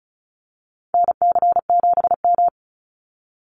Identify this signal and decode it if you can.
Morse code, heard as a keyed tone, about 34 wpm, 700 Hz. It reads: NC7M
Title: Call Signs Repeated 3x - 35wpm